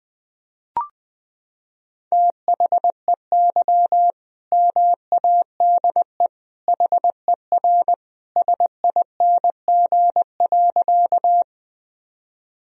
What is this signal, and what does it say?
Morse code, heard as a keyed tone, about 20 wpm, 700 Hz. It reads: THEY MADE HER SING.
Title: Sentences from Top 1000 Words - Review Part 2 - 20wpm